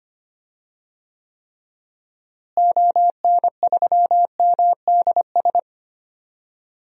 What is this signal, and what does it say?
Morse code, heard as a keyed tone, about 25 wpm, 700 Hz. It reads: ON3MDS
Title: DXCC-100 Call Signs - 6 Characters - Repeated 2x - 25wpm